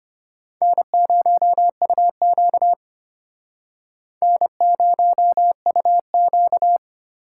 Morse code, heard as a keyed tone, about 33 wpm, 700 Hz. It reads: N0UQ N0UQ